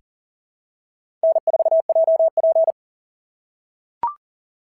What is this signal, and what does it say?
Morse code, heard as a keyed tone, about 40 wpm, 660 Hz. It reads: N4JP
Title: SOTA QSO Practice 40wpm